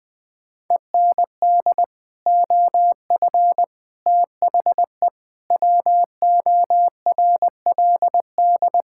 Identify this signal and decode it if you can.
Morse code, heard as a keyed tone, about 20 wpm, 700 Hz. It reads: END OF THE WORLD
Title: Sets of 8 Words - 20wpm